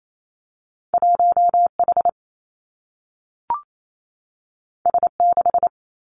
Morse code, heard as a keyed tone, about 28 wpm, 700 Hz. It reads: 1 H S 6
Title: Two Challenging Letter-Number Combinations 28wpm